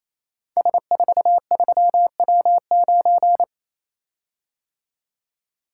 Morse code, heard as a keyed tone, about 27 wpm, 690 Hz. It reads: S43W9
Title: Code Groups - 5 Letter:Numbers - 28wpm